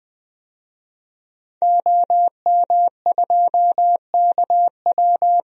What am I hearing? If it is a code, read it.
Morse code, heard as a keyed tone, about 20 wpm, 700 Hz. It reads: OM2KW
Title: DXCC-100 Call Signs - 5 Characters - 20wpm